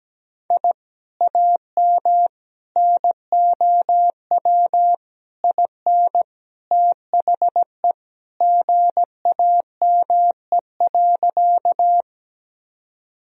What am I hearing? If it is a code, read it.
Morse code, heard as a keyed tone, about 17 wpm, 700 Hz. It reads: I AM NOW IN THE GAME.
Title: Sentences from Top 1000 Words - Review Part 3 - 17wpm